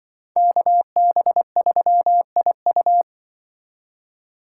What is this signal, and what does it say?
Morse code, heard as a keyed tone, about 26 wpm, 700 Hz. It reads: KB3IU